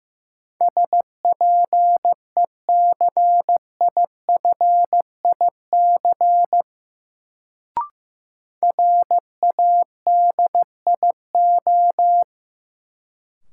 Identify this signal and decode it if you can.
Morse code, heard as a keyed tone, about 16 wpm, 700 Hz. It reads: SPECIFIC RADIO
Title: Sentences from Top 1000 Words - Review Part 3 - New Word Drill - 15wpm